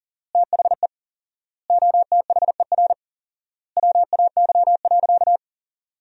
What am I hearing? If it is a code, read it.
Morse code, heard as a keyed tone, about 37 wpm, 700 Hz. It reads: THE OTHER WAY.
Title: 4x Word-Spacing - Sentences from Top 100 Words - 40wpm